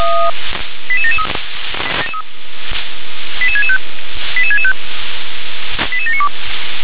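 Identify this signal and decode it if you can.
Five-tone selective call (ZVEI), heard as quick runs of five tones, tones of about 70 ms. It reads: 88773 88665 88665 88792
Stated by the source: ZVEI five-tone selective call received by SDR.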